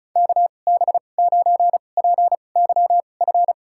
Morse code, heard as a keyed tone, about 34 wpm, spaced as for 25 wpm, 700 Hz. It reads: KB9PYF
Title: USA Call Signs - 6 Characters - 2x Character Spacing - 35wpm